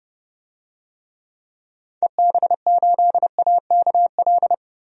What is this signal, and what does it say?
Morse code, heard as a keyed tone, about 30 wpm, 700 Hz. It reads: EB8AKL
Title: DXCC-100 Call Signs - 6 Characters - 30wpm